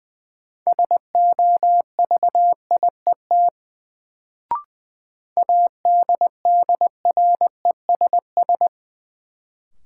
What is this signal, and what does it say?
Morse code, heard as a keyed tone, about 20 wpm, 700 Hz. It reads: SOVIET ADDRESS
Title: Sets of 5 Words Encore - New Word Drill - 20wpm